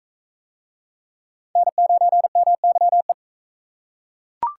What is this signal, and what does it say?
Morse code, heard as a keyed tone, about 37 wpm, 700 Hz. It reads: N9MYE